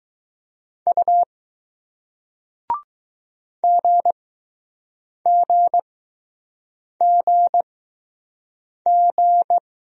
Morse code, heard as a keyed tone, about 20 wpm, spaced as for 6 wpm, 700 Hz. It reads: U GGGG